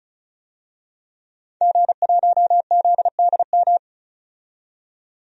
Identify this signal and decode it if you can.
Morse code, heard as a keyed tone, about 36 wpm, 700 Hz. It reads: G1ZDM
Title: DXCC-100 Call Signs - 5 Characters - 35wpm